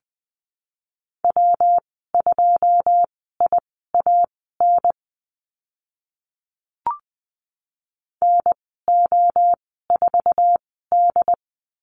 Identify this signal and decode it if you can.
Morse code, heard as a keyed tone, about 20 wpm, spaced as for 14 wpm, 700 Hz. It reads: W2IAN NO4D
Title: 128 Lesson 21g - Number 6 - Call Signs 20wpm